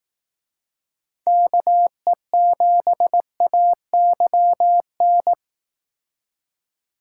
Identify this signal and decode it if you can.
Morse code, heard as a keyed tone, about 18 wpm, 700 Hz. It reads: KE7AYN
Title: Speed Racing - USA 2x3 Call Signs 15wpm